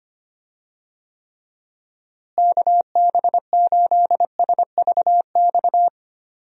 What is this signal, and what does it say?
Morse code, heard as a keyed tone, about 25 wpm, 700 Hz. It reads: KB8SVX
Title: USA Call Signs - 6 Characters - Repeated 2x - 25wpm